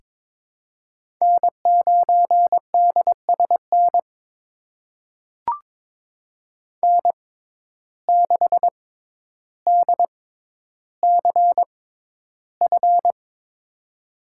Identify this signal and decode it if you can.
Morse code, heard as a keyed tone, about 22 wpm, spaced as for 11 wpm, 700 Hz. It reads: N9DSN N6DCF
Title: USA Call Signs - 5 Characters - 6x Character Spacing - 22wpm